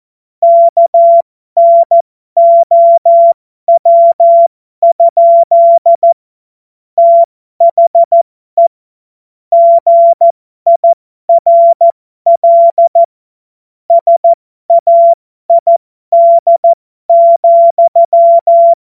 Morse code, heard as a keyed tone, about 14 wpm, spaced as for 12 wpm, 680 Hz. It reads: KNOW? THE GIRL SAID,
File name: Morsekurs / Witze